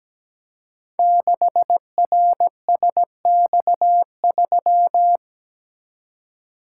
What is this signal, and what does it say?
Morse code, heard as a keyed tone, about 17 wpm, 700 Hz. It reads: 6RSX3
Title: Code Groups - 5 Letter:Numbers - 17wpm